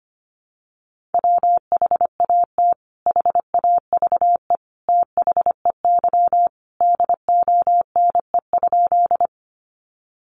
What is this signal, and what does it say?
Morse code, heard as a keyed tone, about 25 wpm, 700 Hz. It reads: WHAT HAVE THEY DONE?
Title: Sentences from Top 500 Words 25wpm